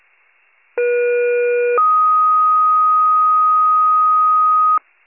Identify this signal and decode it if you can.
Two-tone page, heard as a one-second tone, then a three-second tone, about 480 then 1250 Hz. It is heard from 0.8 to 4.8 s.
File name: pager / 23 FIRE / wcalert